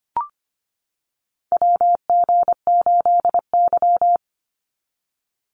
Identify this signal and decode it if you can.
Morse code, heard as a keyed tone, about 25 wpm, 700 Hz. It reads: WG8Y
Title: NA SOTA Chaser Callsigns 25wpm